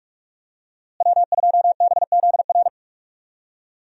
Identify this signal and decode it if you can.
Morse code, heard as a keyed tone, about 46 wpm, 700 Hz. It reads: W2BZR
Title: USA Call Signs - 5 Characters - Repeated 3x - 45wpm